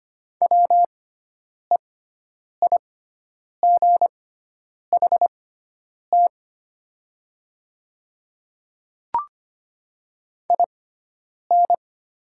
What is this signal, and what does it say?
Morse code, heard as a keyed tone, about 25 wpm, spaced as for 9 wpm, 700 Hz. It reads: WEIGHT IN
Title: 6x Character Spacing - Top 500 Words - 25wpm